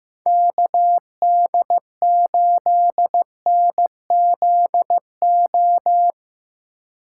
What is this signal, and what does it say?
Morse code, heard as a keyed tone, about 15 wpm, 700 Hz. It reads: KD8NZO
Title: USA Call Signs - 6 Characters - 2x Character Spacing - 15wpm